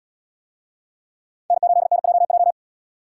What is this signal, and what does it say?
Morse code, heard as a keyed tone, about 70 wpm, 700 Hz. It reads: N8NJJ